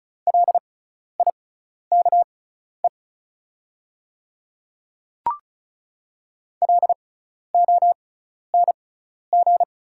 Morse code, heard as a keyed tone, about 36 wpm, spaced as for 12 wpm, 700 Hz. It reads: LIKE LONG
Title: 6x Character Spacing - Top 200 Words - 35wpm